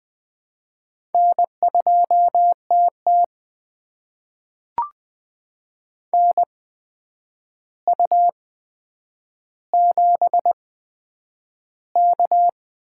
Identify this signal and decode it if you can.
Morse code, heard as a keyed tone, about 20 wpm, spaced as for 9 wpm, 700 Hz. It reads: N2TT NU7K
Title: USA Call Signs - 8x Character Spacing - 20wpm